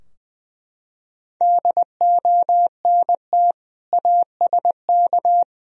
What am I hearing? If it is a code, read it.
Morse code, heard as a keyed tone, about 20 wpm, 700 Hz. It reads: DONT ASK
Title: Sentences from Top 1000 Words - Review Part 3 - 20wpm